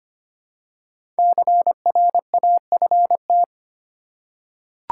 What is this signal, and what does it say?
Morse code, heard as a keyed tone, about 25 wpm, 700 Hz. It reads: CRAFT
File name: 5 Letter Words - First 3 Letters Given - 25wpm